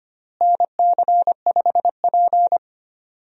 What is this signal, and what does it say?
Morse code, heard as a keyed tone, about 25 wpm, 700 Hz. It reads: NC5P
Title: USA Call Signs - 4 Characters - Repeated 2x - 25wpm